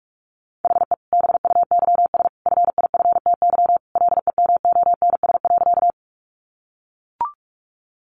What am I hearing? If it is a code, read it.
Morse code, heard as a keyed tone, about 44 wpm, 700 Hz. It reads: HE BUYS FIFTY LEMONS.
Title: Additional Easy Phrases 45wpm